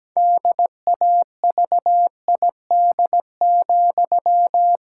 Morse code, heard as a keyed tone, about 17 wpm, 700 Hz. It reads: DAVID,
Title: Pangram Challenge v2 - 17wpm